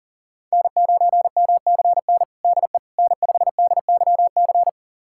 Morse code, heard as a keyed tone, about 40 wpm, 700 Hz. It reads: N9MCN DE N5DYC
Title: USA Callsign DE Callsign - 5 Characters - 40wpm